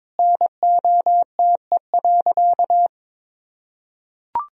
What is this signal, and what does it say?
Morse code, heard as a keyed tone, about 22 wpm, 700 Hz. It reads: NOTE.
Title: Sentences from Top 1000 Words - Review Part 2 - 22wpm